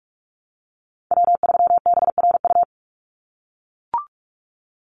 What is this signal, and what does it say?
Morse code, heard as a keyed tone, about 44 wpm, 700 Hz. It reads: W3BRU
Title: USA Call Signs 45wpm